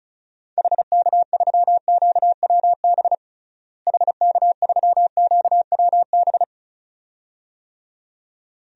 Morse code, heard as a keyed tone, about 34 wpm, 700 Hz. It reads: HK3QWB HK3QWB
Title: DXCC-100 Call Signs - 6 Characters - Repeated 2x - 35wpm